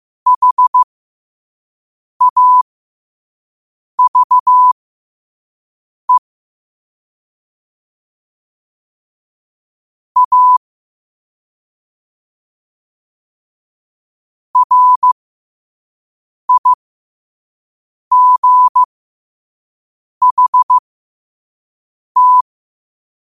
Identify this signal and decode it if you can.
Morse code, heard as a keyed tone, about 15 wpm, spaced as for 5 wpm, 1000 Hz. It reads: HAVE A RIGHT